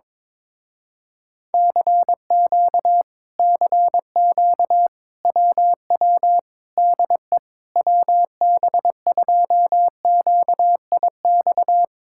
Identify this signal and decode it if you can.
Morse code, heard as a keyed tone, about 22 wpm, 700 Hz. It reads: CQ CQ WW DE WB2QIX
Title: Speed-Racing Common QSO Phrases - 22wpm